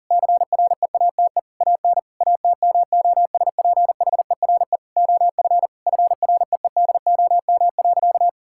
Morse code, heard as a keyed tone, about 40 wpm, 700 Hz. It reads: CREATE AN ATMOSPHERE OF FREEDOM.
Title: Sentences from Top 1600 Words 40wpm